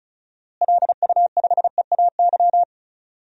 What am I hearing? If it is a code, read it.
Morse code, heard as a keyed tone, about 34 wpm, 700 Hz. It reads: LU5EAY
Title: DXCC-100 Call Signs - 6 Characters - Repeated 3x - 35wpm